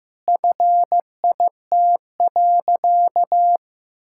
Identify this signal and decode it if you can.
Morse code, heard as a keyed tone, about 15 wpm, 700 Hz. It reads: FIT.
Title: Sentences from Top 1000 Words - Review Part 5 - 15wpm